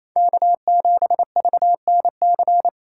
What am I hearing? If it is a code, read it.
Morse code, heard as a keyed tone, about 30 wpm, 700 Hz. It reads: K7VNC
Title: Speed Racing - USA 1x3 Call Signs 28wpm